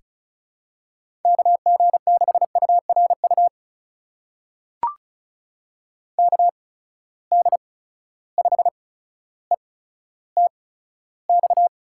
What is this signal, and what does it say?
Morse code, heard as a keyed tone, about 36 wpm, spaced as for 15 wpm, 700 Hz. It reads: KG6URU KD5ETX
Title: USA Call Signs - 6 Characters - 8x Character Spacing - 35wpm